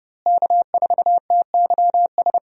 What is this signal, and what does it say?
Morse code, heard as a keyed tone, about 30 wpm, 700 Hz. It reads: K4TYS
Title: USA Call Signs - 5 Characters - Repeated 2x - 30wpm